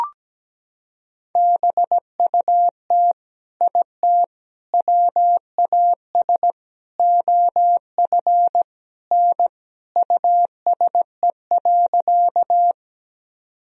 Morse code, heard as a keyed tone, about 17 wpm, 700 Hz. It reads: BUT IT WAS OF N USE.
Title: Sentences from Top 100 Words and Abbreviations 17wpm